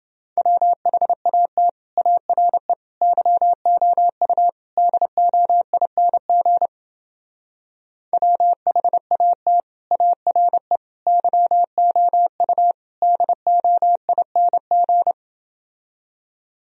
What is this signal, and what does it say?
Morse code, heard as a keyed tone, about 29 wpm, 700 Hz. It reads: WHAT ARE YOU DOING WHAT ARE YOU DOING